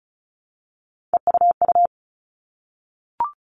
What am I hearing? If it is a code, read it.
Morse code, heard as a keyed tone, about 34 wpm, 700 Hz. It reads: EUU
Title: All Cut Numbers - 3-Digit 35wpm